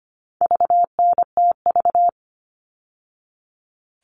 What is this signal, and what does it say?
Morse code, heard as a keyed tone, about 25 wpm, 700 Hz. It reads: VNTV
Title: All Cut Numbers - 4-Digit 25wpm